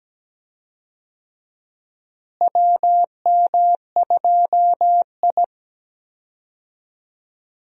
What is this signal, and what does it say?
Morse code, heard as a keyed tone, about 17 wpm, 700 Hz. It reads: WM2I
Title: USA Call Signs - 4 Characters - Repeated 2x - 17wpm